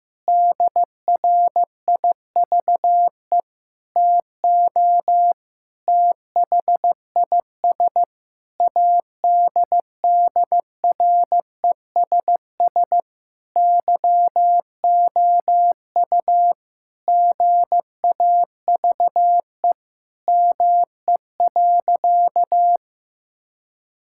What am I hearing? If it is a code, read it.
Morse code, heard as a keyed tone, about 15 wpm, 700 Hz. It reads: DRIVE TO THIS ADDRESS YOU GAVE ME.
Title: Sentences from Top 1000 Words - Review Part 6 - 15wpm